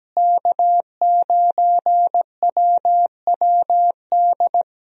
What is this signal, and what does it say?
Morse code, heard as a keyed tone, about 17 wpm, 700 Hz. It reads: K9WWD
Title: USA Call Signs - 5 Characters - Repeated 3x - 17wpm